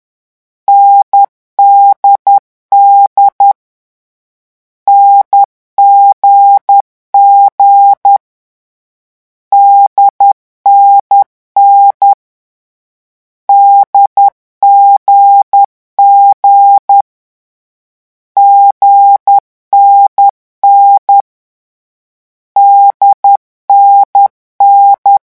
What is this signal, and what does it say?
Morse code, heard as a keyed tone, about 10 wpm, 800 Hz. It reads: NDD NGG DNN DGG GNN DNN
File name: morse mp3 / lecon4-vitesse 10